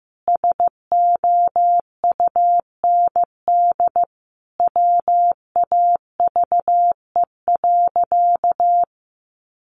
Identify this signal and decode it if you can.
Morse code, heard as a keyed tone, about 15 wpm, 700 Hz. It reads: SOUND WAVE.
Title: Sentences from Top 1400 Words 15wpm